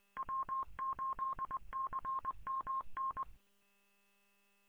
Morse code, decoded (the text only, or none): W8CMN